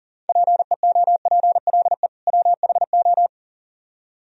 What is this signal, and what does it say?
Morse code, heard as a keyed tone, about 51 wpm, 700 Hz. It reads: PEOPLE WHO